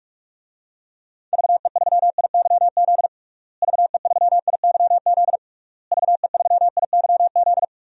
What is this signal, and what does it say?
Morse code, heard as a keyed tone, about 46 wpm, 700 Hz. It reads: VE3IYZ VE3IYZ VE3IYZ
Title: DXCC-100 Call Signs - 6 Characters - Repeated 3x - 45wpm